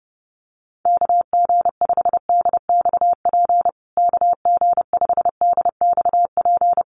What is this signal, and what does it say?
Morse code, heard as a keyed tone, about 30 wpm, 700 Hz. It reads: KG5DXP KG5DXP
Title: USA Call Signs Repeated 3x 30wpm